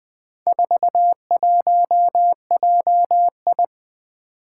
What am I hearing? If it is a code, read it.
Morse code, heard as a keyed tone, about 20 wpm, 700 Hz. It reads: 41JI
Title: Four Challenging Letter-Number Combinations 20wpm